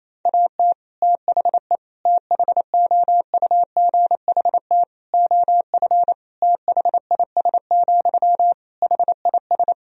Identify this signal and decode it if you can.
Morse code, heard as a keyed tone, about 28 wpm, 700 Hz. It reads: AT THE THOUGHT OF THIS, HIS
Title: Sentences from Top 1000 Words - Review Part 2 - 28wpm